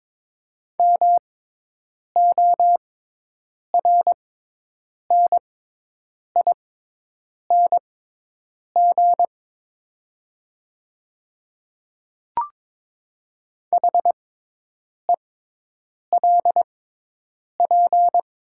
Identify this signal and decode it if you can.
Morse code, heard as a keyed tone, about 22 wpm, spaced as for 8 wpm, 700 Hz. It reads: MORNING HELP
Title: 6x Character Spacing - Top 500 Words - 22wpm